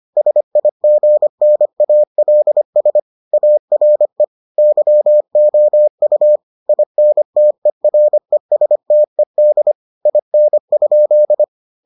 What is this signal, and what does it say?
Morse code, heard as a keyed tone, about 25 wpm, 600 Hz. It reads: SIGNALS ARE YOU INTERESTED IN?